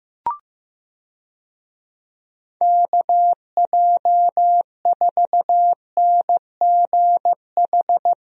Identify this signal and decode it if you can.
Morse code, heard as a keyed tone, about 15 wpm, 700 Hz. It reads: KJ4NGH
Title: USA Call Signs - 6 Characters - Repeated 2x - 15wpm